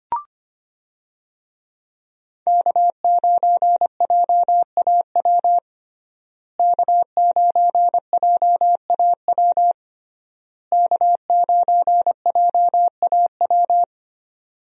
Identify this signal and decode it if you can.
Morse code, heard as a keyed tone, about 25 wpm, 700 Hz. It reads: K9JAW K9JAW K9JAW
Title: USA Call Signs - 5 Characters - Repeated 3x - 25wpm